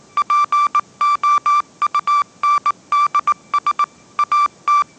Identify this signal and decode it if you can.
Morse code, heard as a keyed tone, about 19 wpm, 1170 Hz. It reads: POUNDS AT
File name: 20wpm